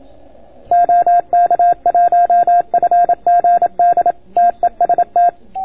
Morse code, heard as a keyed tone, about 27 wpm, 690 Hz. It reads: OK1FGD TEST